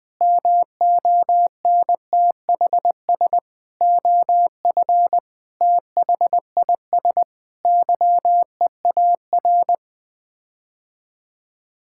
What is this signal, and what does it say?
Morse code, heard as a keyed tone, about 20 wpm, 700 Hz. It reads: MONTHS OF THIS YEAR
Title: Sets of 8 Words - 20wpm